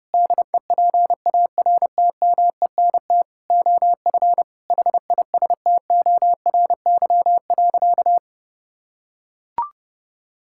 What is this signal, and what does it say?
Morse code, heard as a keyed tone, about 32 wpm, 700 Hz. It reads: DEPARTMENT OF HISTORY.
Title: Sentences from Top 1000 Words - Review Part 5 - 30wpm